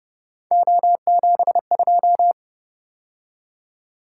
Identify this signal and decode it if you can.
Morse code, heard as a keyed tone, about 29 wpm, 700 Hz. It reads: O72
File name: Speed-Racing - Three Letter-Number Combinations - 25wpm